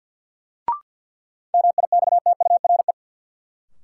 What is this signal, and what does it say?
Morse code, heard as a keyed tone, about 51 wpm, 700 Hz. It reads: MIXTURE
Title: Sentences from Top 1500 Words - New Word Drill - 50wpm